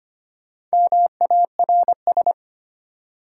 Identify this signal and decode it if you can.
Morse code, heard as a keyed tone, about 25 wpm, 700 Hz. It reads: MARS
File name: practice / 25wpm / Easy Words with Hints - 25wpm